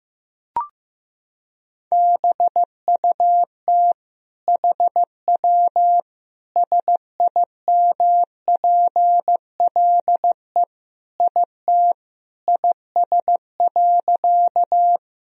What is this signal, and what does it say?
Morse code, heard as a keyed tone, about 15 wpm, 700 Hz. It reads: BUT HW SIMPLE IT IS.
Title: Sentences from Top 400 Words and Abbreviations 15wpm